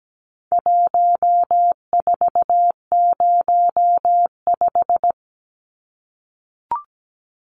Morse code, heard as a keyed tone, about 17 wpm, 700 Hz. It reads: 1405
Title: Numbers - 4-Digit 17wpm